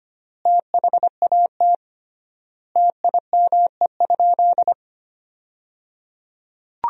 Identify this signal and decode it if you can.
Morse code, heard as a keyed tone, about 25 wpm, 700 Hz. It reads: THAT TIME?
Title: 3x Word-Spacing - Sentences from Top 200 Words - 25wpm